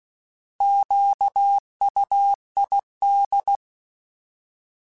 Morse code, heard as a keyed tone, about 16 wpm, 800 Hz. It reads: QUID